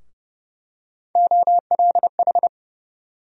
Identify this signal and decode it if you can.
Morse code, heard as a keyed tone, about 32 wpm, 700 Hz. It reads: OLH
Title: USA Call Sign Postfixes - 3 Letters 30wpm